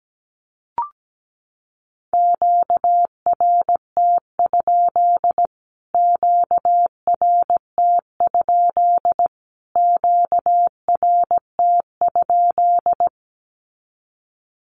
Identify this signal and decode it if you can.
Morse code, heard as a keyed tone, about 17 wpm, 700 Hz. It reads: QRT? QRT? QRT?
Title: Q-Codes Repeated 3x 17wpm